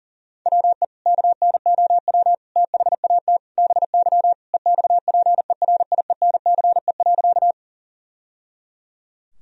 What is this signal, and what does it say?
Morse code, heard as a keyed tone, about 40 wpm, 700 Hz. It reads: WE KNOW THAT BY EXPERIENCE.
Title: Sentences from Top 1000 Words - Review Part 3 - 40wpm